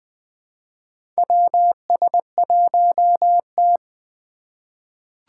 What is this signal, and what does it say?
Morse code, heard as a keyed tone, about 20 wpm, 700 Hz. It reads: WS1T
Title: USA Call Signs - 2x Character Spacing - 20wpm